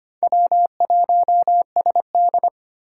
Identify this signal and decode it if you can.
Morse code, heard as a keyed tone, about 25 wpm, 700 Hz. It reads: W1SD
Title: USA Call Signs - 2x Character Spacing - 25wpm